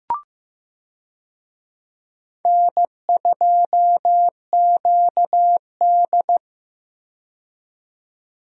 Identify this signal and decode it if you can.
Morse code, heard as a keyed tone, about 15 wpm, 700 Hz. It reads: N2QD
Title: USA Call Signs - 4 Characters - 15wpm